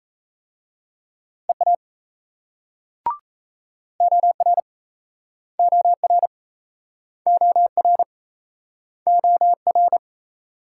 Morse code, heard as a keyed tone, about 38 wpm, 700 Hz. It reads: EA OR OR OR OR